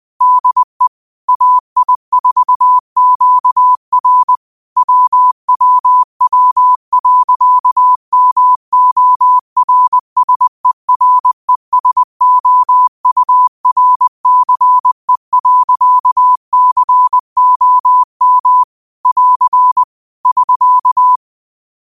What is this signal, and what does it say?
Morse code, heard as a keyed tone, about 20 wpm, 1000 Hz. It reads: DE AI4QR WWW.MORSERESOURCE.COM <AR> <SK>
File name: qotdfiles / 20wpm